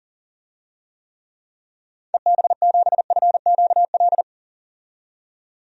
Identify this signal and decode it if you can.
Morse code, heard as a keyed tone, about 40 wpm, 700 Hz. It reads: EB7FQL
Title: DXCC-100 Call Signs - 6 Characters - 40wpm